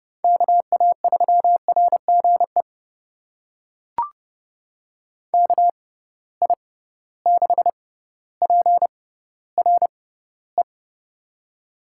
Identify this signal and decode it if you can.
Morse code, heard as a keyed tone, about 30 wpm, spaced as for 15 wpm, 700 Hz. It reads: KA3RGE KI6PRE
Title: USA Call Signs - 6 Characters - 6x Character Spacing - 30wpm